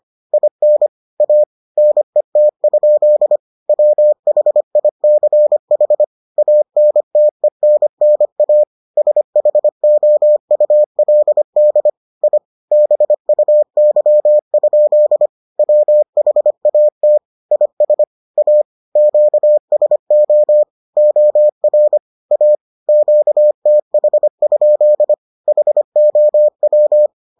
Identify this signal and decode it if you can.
Morse code, heard as a keyed tone, about 25 wpm, 600 Hz. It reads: IN A NET? WHICH ANTENNA SHOULD I BUY? WHAT IS A QSO OR A QTH? HOW